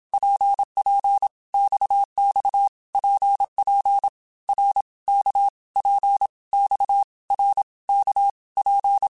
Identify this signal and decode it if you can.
Morse code, heard as a keyed tone, about 27 wpm, 800 Hz. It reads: PP XX PP R K P X R K P
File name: Basic Letter Groups By7Groups / RKPX / RKPX@25